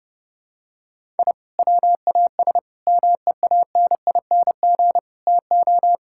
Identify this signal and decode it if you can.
Morse code, heard as a keyed tone, about 32 wpm, 700 Hz. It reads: I WAS MEANING TO
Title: Sentences from Top 1000 Words - Review Part 6 - 30wpm